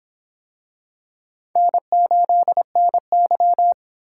Morse code, heard as a keyed tone, about 25 wpm, 700 Hz. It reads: N8NY